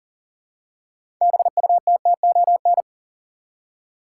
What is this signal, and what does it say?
Morse code, heard as a keyed tone, about 40 wpm, 700 Hz. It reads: BUTTON
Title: Six Letter Words - 40wpm